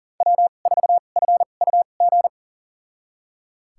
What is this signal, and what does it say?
Morse code, heard as a keyed tone, about 40 wpm, spaced as for 29 wpm, 700 Hz. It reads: W4FUG
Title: USA Call Signs - 5 Characters - 2x Character Spacing - 40wpm